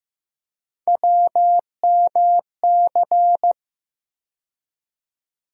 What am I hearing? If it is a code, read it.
Morse code, heard as a keyed tone, about 15 wpm, 700 Hz. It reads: WMC
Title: USA Call Sign Postfixes - 3 Letters 15wpm